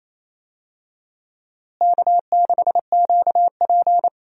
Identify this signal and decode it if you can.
Morse code, heard as a keyed tone, about 27 wpm, 700 Hz. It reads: K6QP
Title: USA Call Signs - 4 Characters - Repeated 2x - 28wpm